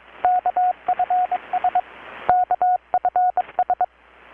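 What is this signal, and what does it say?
Morse code, heard as a keyed tone, about 22 wpm, 720 Hz, with radio noise. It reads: KFS KFS
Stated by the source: Morse from KPH's Night of Nights XX maritime radiotelegraphy event, received in Chiapas, Mexico about 3,800 km away on a Kenwood R-600 with a grounded 10 m long-wire antenna.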